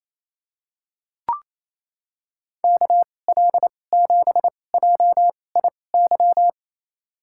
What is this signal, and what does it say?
Morse code, heard as a keyed tone, about 28 wpm, spaced as for 20 wpm, 700 Hz. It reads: KL7JIY
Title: USA Call Signs - 6 Characters - 2x Character Spacing - 28wpm